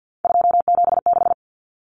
Morse code, heard as a keyed tone, about 50 wpm, 700 Hz. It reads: 276
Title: Common Cut Numbers - 3-Digit 50wpm